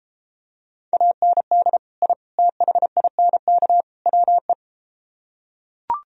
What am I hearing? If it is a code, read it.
Morse code, heard as a keyed tone, about 32 wpm, 700 Hz. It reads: AND I THINK WE